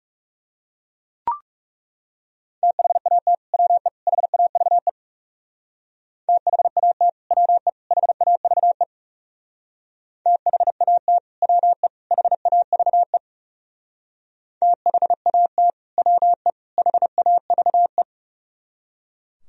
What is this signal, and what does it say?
Morse code, heard as a keyed tone, about 36 wpm, 700 Hz. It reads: THAT WE HAVE THAT WE HAVE THAT WE HAVE THAT WE HAVE
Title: Speed-Racing - Sets of 3 Words 30wpm